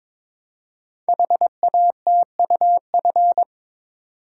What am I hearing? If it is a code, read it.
Morse code, heard as a keyed tone, about 23 wpm, 700 Hz. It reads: HATUF
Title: Code Groups - 5 Letters - 22wpm